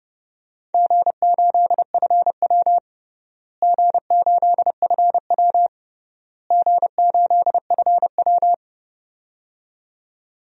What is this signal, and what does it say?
Morse code, heard as a keyed tone, about 32 wpm, 700 Hz. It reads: G8FW G8FW G8FW
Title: DXCC-100 Call Signs - 4 Characters - Repeated 3x - 30wpm